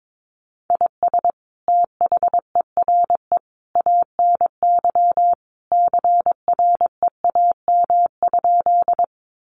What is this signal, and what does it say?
Morse code, heard as a keyed tone, about 22 wpm, 700 Hz. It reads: IS THERE ANY CREAM?
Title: Sentences from Top 1500 Words 22wpm